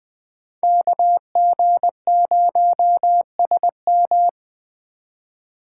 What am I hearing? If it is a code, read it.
Morse code, heard as a keyed tone, about 20 wpm, 700 Hz. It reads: KG0SM
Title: Speed Racing - USA Call Signs 17wpm